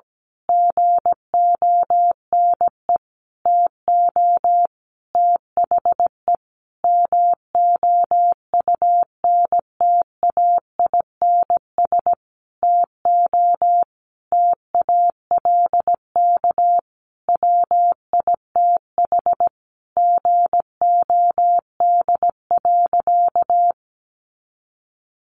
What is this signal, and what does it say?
Morse code, heard as a keyed tone, about 17 wpm, 700 Hz. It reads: GONE TO THE MOUNTAINS TO TALK WITH GOD.
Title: A Sound Of Thunder 17wpm